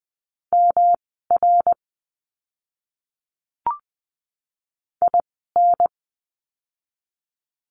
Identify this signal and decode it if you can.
Morse code, heard as a keyed tone, about 20 wpm, spaced as for 14 wpm, 700 Hz. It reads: MR IN
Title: 244 Lesson 39f - Letter z - Words 20wpm